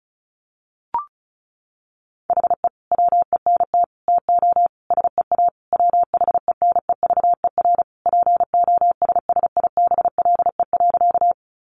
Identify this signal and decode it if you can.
Morse code, heard as a keyed tone, about 34 wpm, 700 Hz. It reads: HE WENT TO SEA WHENEVER POSSIBLE.
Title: Sentences from Top 1200 Words 35wpm